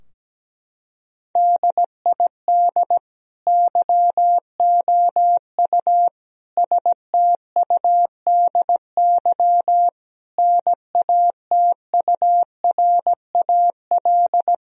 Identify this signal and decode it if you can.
Morse code, heard as a keyed tone, about 17 wpm, 700 Hz. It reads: DID YOU STUDY NATURAL
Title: Sentences from Top 1000 Words - Review Part 4 - 17wpm